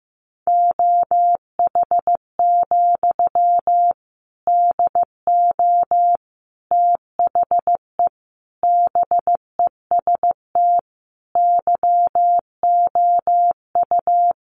Morse code, heard as a keyed tone, about 15 wpm, 700 Hz. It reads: OH, DO THE BEST YOU
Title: Sentences from Top 500 Words 15wpm